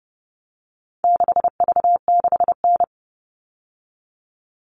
Morse code, heard as a keyed tone, about 30 wpm, 700 Hz. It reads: 6V6N
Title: All Cut Numbers - 4-Digit 30wpm